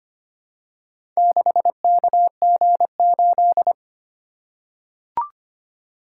Morse code, heard as a keyed tone, about 25 wpm, 700 Hz. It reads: 6KG8
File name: practice / 25wpm / Four Letter-Number Combinations - 25wpm